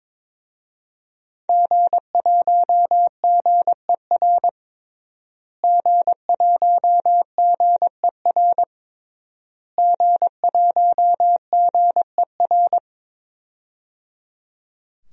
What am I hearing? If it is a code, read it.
Morse code, heard as a keyed tone, about 22 wpm, 700 Hz. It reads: G1GER G1GER G1GER
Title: DXCC-100 Call Signs - 5 Characters - Repeated 3x - 22wpm